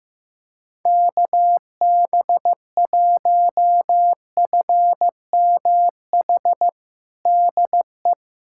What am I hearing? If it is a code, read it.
Morse code, heard as a keyed tone, about 15 wpm, 700 Hz. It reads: KB1FMH DE
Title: USA Callsign DE Callsign - 6 Characters - 15wpm